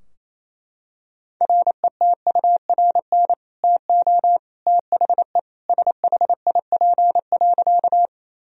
Morse code, heard as a keyed tone, about 27 wpm, 700 Hz. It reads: RETURN TO THE SHIP.
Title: Sentences from Top 1000 Words - Review Part 2 - 28wpm